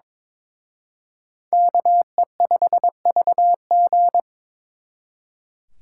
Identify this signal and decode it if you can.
Morse code, heard as a keyed tone, about 26 wpm, 700 Hz. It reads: KE5VG
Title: Speed Racing - USA 2x2 Call Signs 22wpm